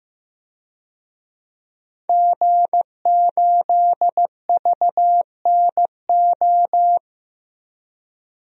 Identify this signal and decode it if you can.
Morse code, heard as a keyed tone, about 15 wpm, 700 Hz. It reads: G8VNO
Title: DXCC-100 Call Signs - 5 Characters - 15wpm